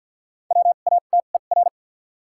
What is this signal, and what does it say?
Morse code, heard as a keyed tone, about 53 wpm, spaced as for 37 wpm, 700 Hz. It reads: WATER
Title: 2x Character Spacing - Top 200 Words - 50wpm